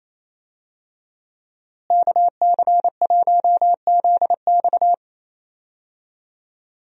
Morse code, heard as a keyed tone, about 28 wpm, 700 Hz. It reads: KC1ZX
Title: USA Call Signs - 5 Characters - Repeated 2x - 28wpm